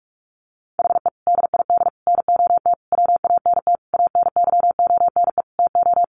Morse code, heard as a keyed tone, about 44 wpm, 700 Hz. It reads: HE DID NOT WANT ANYONE TO
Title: Sentences from Top 1100 Words 45wpm